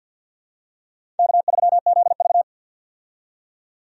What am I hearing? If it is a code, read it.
Morse code, heard as a keyed tone, about 53 wpm, 700 Hz. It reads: K3ZV
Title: USA Call Signs - 4 Characters - 50wpm